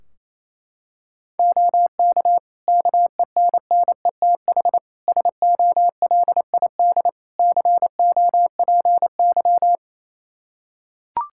Morse code, heard as a keyed tone, about 28 wpm, 700 Hz. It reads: OK KENNETH SOLID COPY
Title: Practice QSO - Exchange 3 - More Info - 28wpm